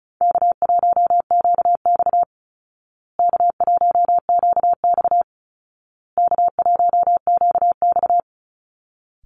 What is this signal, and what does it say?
Morse code, heard as a keyed tone, about 34 wpm, 700 Hz. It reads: K1QX K1QX K1QX
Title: Call Signs Repeated 2x - 35wpm